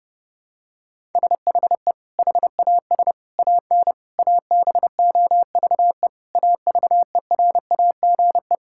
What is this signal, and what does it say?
Morse code, heard as a keyed tone, about 30 wpm, 700 Hz. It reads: SHE HAS AN ABOVE AVERAGE
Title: Sentences from Top 1000 Words - Review Part 6 - 30wpm